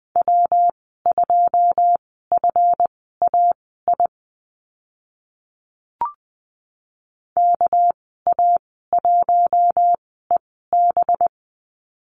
Morse code, heard as a keyed tone, about 20 wpm, spaced as for 14 wpm, 700 Hz. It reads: W2FAI KA1EB
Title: 213 Lesson 34g - Number 8 - Call Signs 20wpm